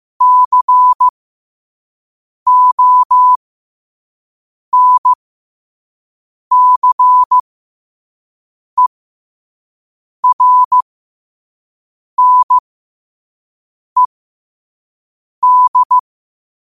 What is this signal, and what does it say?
Morse code, heard as a keyed tone, about 15 wpm, spaced as for 5 wpm, 1000 Hz. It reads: CONCERNED
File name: qotdfiles / 5wpm